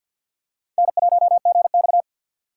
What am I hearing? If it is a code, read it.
Morse code, heard as a keyed tone, about 50 wpm, 700 Hz. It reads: N1GX
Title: USA Call Signs - 3x Character Spacing - 50wpm